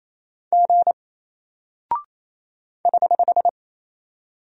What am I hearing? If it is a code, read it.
Morse code, heard as a keyed tone, about 28 wpm, 700 Hz. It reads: G <HH>
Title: Single Character (Letter:Number:Punctuation:Prosign) Alt 28wpm